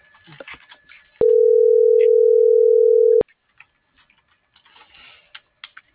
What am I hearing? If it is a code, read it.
Ringback tone (North America), 440 + 480 Hz.